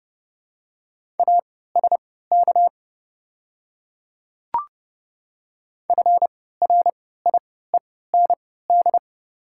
Morse code, heard as a keyed tone, about 30 wpm, spaced as for 17 wpm, 700 Hz. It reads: ASK FRIEND
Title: 3x Character Spacing - Top 500 Words - 30wpm